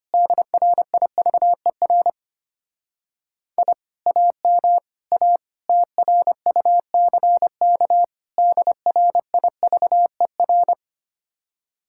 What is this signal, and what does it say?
Morse code, heard as a keyed tone, about 32 wpm, 700 Hz. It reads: DRIVER I AM A TRUCK DRIVER